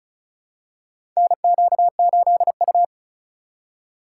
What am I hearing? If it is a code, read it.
Morse code, heard as a keyed tone, about 36 wpm, 700 Hz. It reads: NQ8U
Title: USA Call Signs - 4 Characters - 4x Character Spacing - 35wpm